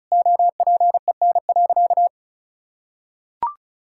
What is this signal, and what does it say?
Morse code, heard as a keyed tone, about 36 wpm, 700 Hz. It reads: OPEN.
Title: Sentences from Top 1600 Words 35wpm